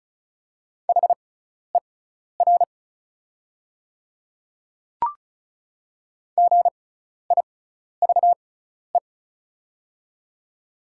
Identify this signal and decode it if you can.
Morse code, heard as a keyed tone, about 36 wpm, spaced as for 12 wpm, 690 Hz. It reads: HER GIVE
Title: 6x Character Spacing - Top 200 Words - 35wpm